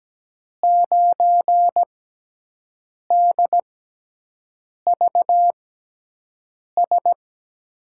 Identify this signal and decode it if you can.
Morse code, heard as a keyed tone, about 17 wpm, spaced as for 6 wpm, 700 Hz. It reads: 9DVS
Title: USA Call Signs - 5 Characters - 6x Character Spacing - 17wpm